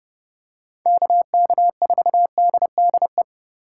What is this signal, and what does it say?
Morse code, heard as a keyed tone, about 30 wpm, 700 Hz. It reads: KK4DDE
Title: USA Call Signs - 6 Characters - 3x Character Spacing - 30wpm